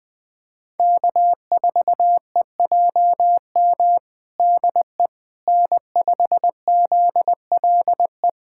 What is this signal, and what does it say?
Morse code, heard as a keyed tone, about 21 wpm, 700 Hz. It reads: K4EJM DE N5ZLE
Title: USA Callsign DE Callsign - 5 Characters - 20wpm